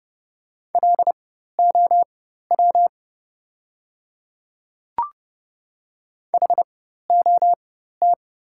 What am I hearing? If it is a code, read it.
Morse code, heard as a keyed tone, about 30 wpm, spaced as for 14 wpm, 690 Hz. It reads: LOW HOT
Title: 4x Character Spacing - Top 500 Words - 30wpm